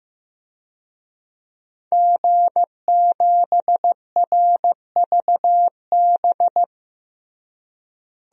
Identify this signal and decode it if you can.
Morse code, heard as a keyed tone, about 15 wpm, 700 Hz. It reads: G7RVB
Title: DXCC-100 Call Signs - 5 Characters - Repeated 2x - 15wpm